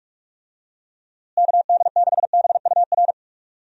Morse code, heard as a keyed tone, about 43 wpm, 700 Hz. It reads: KD6BUR
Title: USA Call Signs - 6 Characters - Repeated 2x - 45wpm